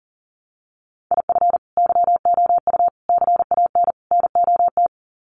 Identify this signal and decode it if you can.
Morse code, heard as a keyed tone, about 40 wpm, 700 Hz. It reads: IF YOU CAN NOT
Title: Sets of 4 Words 40wpm